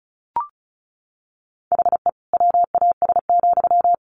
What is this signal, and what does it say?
Morse code, heard as a keyed tone, about 34 wpm, 700 Hz. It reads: HE WAS,
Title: Sentences from Top 1100 Words 35wpm